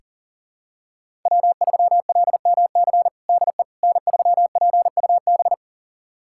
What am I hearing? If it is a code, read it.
Morse code, heard as a keyed tone, about 40 wpm, 700 Hz. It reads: W3LMC DE N3PUB
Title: USA Callsign DE Callsign - 5 Characters - 40wpm